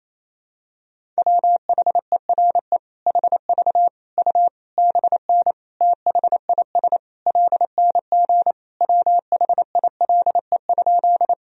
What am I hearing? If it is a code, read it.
Morse code, heard as a keyed tone, about 28 wpm, 700 Hz. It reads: WHERE HV U BN THIS LNG WHILE?
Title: Sentences from Top 300 Words and Abbreviations 28wpm